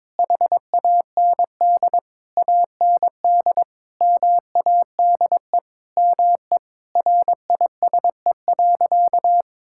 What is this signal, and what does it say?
Morse code, heard as a keyed tone, about 22 wpm, 700 Hz. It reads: HAND AND MADE ME RISE.
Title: Sentences from Top 1000 Words - Review Part 6 - 22wpm